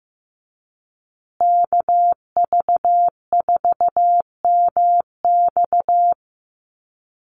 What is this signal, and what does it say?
Morse code, heard as a keyed tone, about 15 wpm, 700 Hz. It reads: KV4MX
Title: USA Call Signs Repeated 3x 15wpm